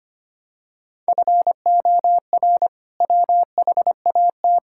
Morse code, heard as a keyed tone, about 25 wpm, 700 Hz. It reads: FOR WHAT
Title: Sentences from Top 1000 Words - Review Part 5 - 25wpm